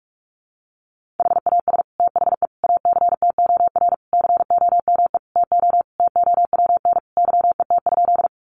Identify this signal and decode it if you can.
Morse code, heard as a keyed tone, about 44 wpm, 700 Hz. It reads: HAS THE ACTOR COME TO TOWN YET?
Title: Sentences from Top 1500 Words 45wpm